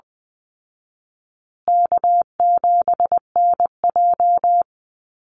Morse code, heard as a keyed tone, about 20 wpm, 700 Hz. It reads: K7NJ
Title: Call Signs Repeated 3x - 20wpm